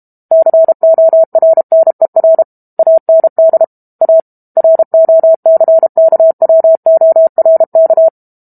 Morse code, heard as a keyed tone, about 32 wpm, 650 Hz. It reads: CORNER AND A ROCKWORK
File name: Book Two / 06-Chapter 6 - 32 WPM